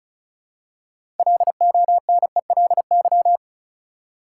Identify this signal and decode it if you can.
Morse code, heard as a keyed tone, about 34 wpm, 700 Hz. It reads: LONELY
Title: Six Letter Words - 35wpm